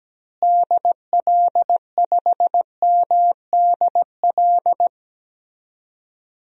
Morse code, heard as a keyed tone, about 17 wpm, 700 Hz. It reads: DL5MDL
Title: DXCC-100 Call Signs - 6 Characters - Repeated 2x - 17wpm